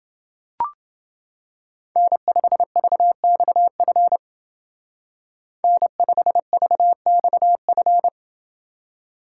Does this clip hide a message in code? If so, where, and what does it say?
Yes: Morse code from 2.0 to 8.1 s, heard as a keyed tone, about 30 wpm, 700 Hz. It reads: N5VXF N5VXF